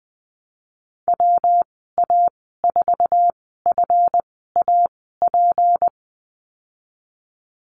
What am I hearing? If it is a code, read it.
Morse code, heard as a keyed tone, about 19 wpm, spaced as for 14 wpm, 700 Hz. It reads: WA4FAP
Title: 154 Lesson 25g - Letter p - Call Signs 20wpm